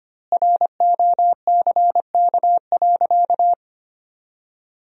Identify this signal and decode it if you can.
Morse code, heard as a keyed tone, about 25 wpm, 700 Hz. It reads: ROCK.
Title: 2x Word-Spacing - Sentences from Top 400 Words - 25wpm